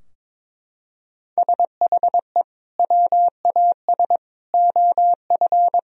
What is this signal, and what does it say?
Morse code, heard as a keyed tone, about 22 wpm, 700 Hz. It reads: SHE WAS OF
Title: Sentences from Top 1000 Words - Review Part 4 - 22wpm